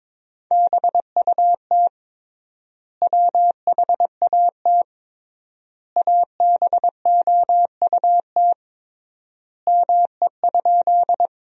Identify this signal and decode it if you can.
Morse code, heard as a keyed tone, about 22 wpm, 700 Hz. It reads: BUT WHAT ABOUT ME?
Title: 3x Word-Spacing - Sentences from Top 200 Words - 22wpm